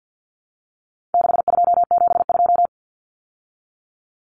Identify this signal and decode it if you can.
Morse code, heard as a keyed tone, about 50 wpm, 700 Hz. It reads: 6272
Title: Numbers - 4-Digit 50wpm